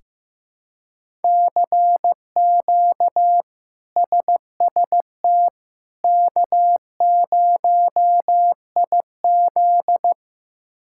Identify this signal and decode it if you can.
Morse code, heard as a keyed tone, about 15 wpm, 700 Hz. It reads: CQ SST K0IZ
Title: K1USN SST QSO Practice - 15wpm